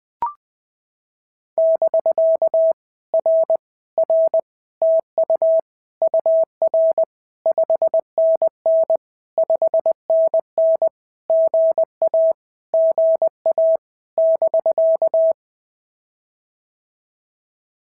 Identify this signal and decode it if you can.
Morse code, heard as a keyed tone, about 20 wpm, 690 Hz. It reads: <BK> R R TU UR 5NN 5NN GA GA <BK>
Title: POTA QSO Practice - 20wpm